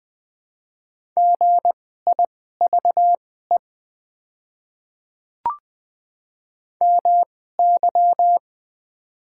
Morse code, heard as a keyed tone, about 20 wpm, spaced as for 14 wpm, 700 Hz. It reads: GIVE MY
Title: 2x Character Spacing - Top 200 Words - 20wpm